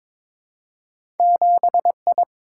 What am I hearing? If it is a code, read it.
Morse code, heard as a keyed tone, about 22 wpm, 700 Hz. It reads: 7I
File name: practice / 22wpm / Two Letter-Number Combinations - 22wpm